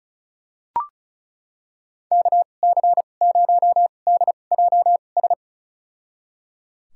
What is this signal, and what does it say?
Morse code, heard as a keyed tone, about 34 wpm, spaced as for 25 wpm, 700 Hz. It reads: KC0DJS
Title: USA Call Signs - 6 Characters - 2x Character Spacing - 35wpm